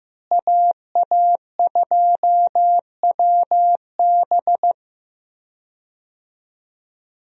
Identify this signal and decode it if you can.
Morse code, heard as a keyed tone, about 15 wpm, 700 Hz. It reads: AA2WB
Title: USA Call Signs - 5 Characters - Repeated 3x - 15wpm